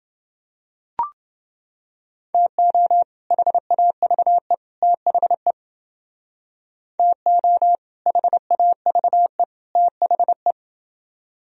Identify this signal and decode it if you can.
Morse code, heard as a keyed tone, about 28 wpm, 700 Hz. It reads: TO HAVE THE TO HAVE THE